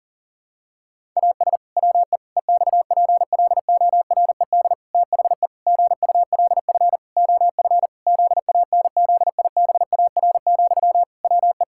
Morse code, heard as a keyed tone, about 40 wpm, 700 Hz. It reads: AS WE EXPLORED THE GULF OF ZANZIBAR, WE
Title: Pangram Challenge v2 - 40wpm